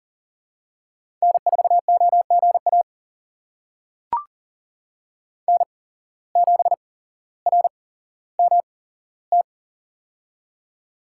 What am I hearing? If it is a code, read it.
Morse code, heard as a keyed tone, about 40 wpm, spaced as for 17 wpm, 700 Hz. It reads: N4OGA N7RMT
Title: USA Call Signs - 5 Characters - 8x Character Spacing - 40wpm